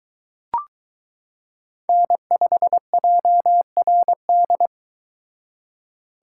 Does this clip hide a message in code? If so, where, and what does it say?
Yes: Morse code from 1.9 to 4.7 s, heard as a keyed tone, about 22 wpm, 700 Hz. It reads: N5JRD